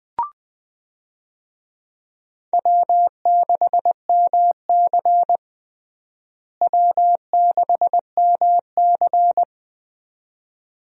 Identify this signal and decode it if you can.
Morse code, heard as a keyed tone, about 20 wpm, 700 Hz. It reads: W6MC W6MC
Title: USA Call Signs - 4 Characters - Repeated 2x - 20wpm